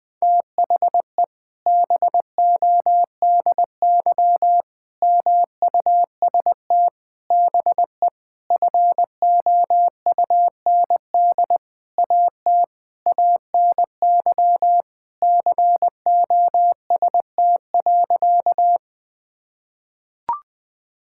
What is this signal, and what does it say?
Morse code, heard as a keyed tone, about 20 wpm, 700 Hz. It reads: THE BODY MUST BE FOUND AT ANY COST.
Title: Sentences from Top 1000 Words - Review Part 5 - 20wpm